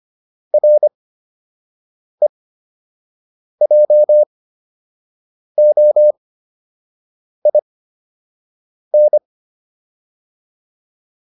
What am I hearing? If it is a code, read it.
Morse code, heard as a keyed tone, about 25 wpm, spaced as for 6 wpm, 600 Hz. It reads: REJOIN